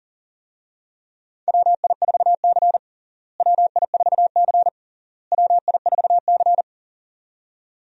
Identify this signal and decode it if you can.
Morse code, heard as a keyed tone, about 40 wpm, 700 Hz. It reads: WI4C WI4C WI4C
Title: USA Call Signs - 4 Characters - Repeated 3x - 40wpm